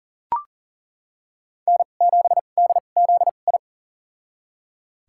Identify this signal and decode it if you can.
Morse code, heard as a keyed tone, about 37 wpm, spaced as for 28 wpm, 700 Hz. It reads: N7DZI
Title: USA Call Signs - 5 Characters - 2x Character Spacing - 40wpm